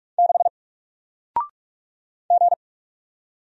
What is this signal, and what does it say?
Morse code, heard as a keyed tone, about 44 wpm, 700 Hz. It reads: 6 G
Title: Single Letter-Number Alt 45wpm